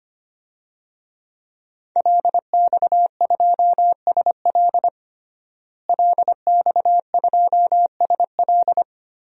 Morse code, heard as a keyed tone, about 25 wpm, 700 Hz. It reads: LX2SL LX2SL
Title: DXCC-100 Call Signs - 5 Characters - Repeated 3x - 25wpm